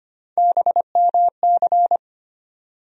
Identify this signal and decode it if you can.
Morse code, heard as a keyed tone, about 25 wpm, 700 Hz. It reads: BMC
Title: USA Call Sign Postfixes - 3 Letters 25wpm